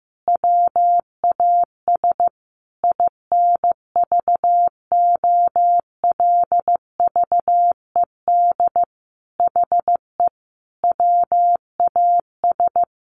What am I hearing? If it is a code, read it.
Morse code, heard as a keyed tone, about 15 wpm, 700 Hz. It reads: WAS INVOLVED HE WAS
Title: Sentences from Top 1400 Words 15wpm